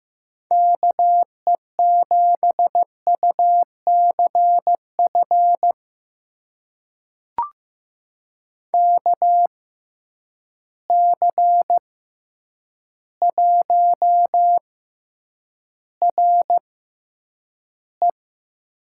Morse code, heard as a keyed tone, about 15 wpm, spaced as for 8 wpm, 700 Hz. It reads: KE7UCF KC1RE
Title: USA Call Signs - 6x Character Spacing - 15wpm